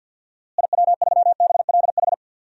Morse code, heard as a keyed tone, about 47 wpm, 700 Hz. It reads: IW3BLH
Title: DXCC-100 Call Signs - 6 Characters - Repeated 3x - 50wpm